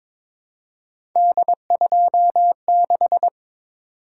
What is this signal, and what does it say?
Morse code, heard as a keyed tone, about 22 wpm, 700 Hz. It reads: D26
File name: practice / 22wpm / Three Letter-Number Combinations - 22wpm